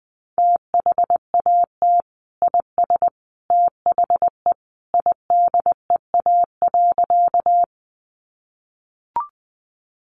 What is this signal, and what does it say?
Morse code, heard as a keyed tone, about 20 wpm, 700 Hz. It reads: THAT IS THE IDEA.
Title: Sentences from Top 300 Words 20wpm